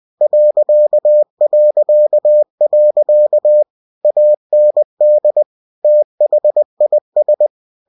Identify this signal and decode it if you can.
Morse code, heard as a keyed tone, about 20 wpm, 600 Hz. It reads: ... AND THIS
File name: cw-books / winnie-the-pooh / winnie-20wpm-page-0052